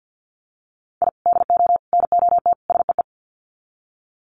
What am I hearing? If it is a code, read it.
Morse code, heard as a keyed tone, about 50 wpm, 700 Hz. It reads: I DO NOT SEE
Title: Sets of 4 Words 50wpm